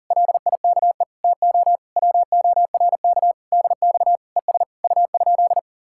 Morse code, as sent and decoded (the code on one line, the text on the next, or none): .-.. .. -.- . / - --- / .-- --- .-. -.- / -.. -..- / . ... / ..- ..--..
LIKE TO WORK DX ES U?